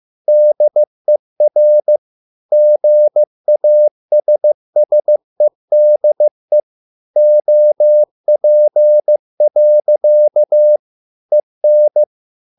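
Morse code, heard as a keyed tone, about 15 wpm, 600 Hz. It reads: DER GASSEDE OP. EN
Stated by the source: Morse code at a slow 15 wpm from CW med Gnister.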